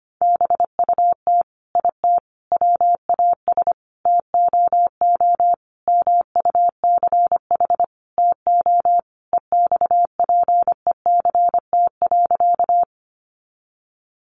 Morse code, heard as a keyed tone, about 25 wpm, 700 Hz. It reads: BUT IT WAS TOO MUCH TO EXPECT.
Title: Sentences from Top 1100 Words 25wpm